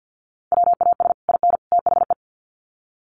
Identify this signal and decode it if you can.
Morse code, heard as a keyed tone, about 48 wpm, 700 Hz. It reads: WAS IN THE
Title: Sets of 3 Words 50wpm